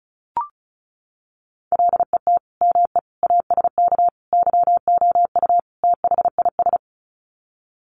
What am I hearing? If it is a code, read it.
Morse code, heard as a keyed tone, about 34 wpm, 700 Hz. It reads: LET ME ASK YOU THIS
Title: Sets of 5 Words 35wpm